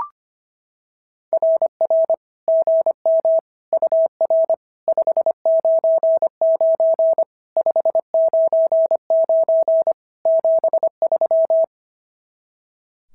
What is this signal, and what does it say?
Morse code, heard as a keyed tone, about 24 wpm, 650 Hz. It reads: RR GM UR 599 599 73
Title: SOTA QSO Practice 25wpm